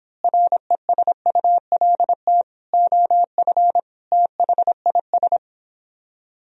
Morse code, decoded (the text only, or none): RESULT OF THIS